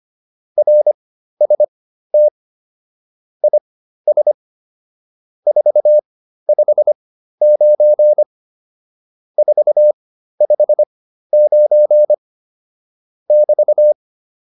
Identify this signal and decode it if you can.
Morse code, heard as a keyed tone, about 25 wpm, spaced as for 13 wpm, 600 Hz. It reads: RST IS 459 459 <BT>
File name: QSO307 13